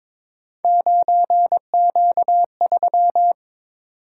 Morse code, heard as a keyed tone, about 22 wpm, 700 Hz. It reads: 9Q3
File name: practice / 22wpm / Three Letter-Number Combinations - 22wpm